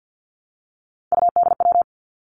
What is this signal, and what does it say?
Morse code, heard as a keyed tone, about 50 wpm, 700 Hz. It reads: UDW
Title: Three Challenging Letter-Number Combinations 50wpm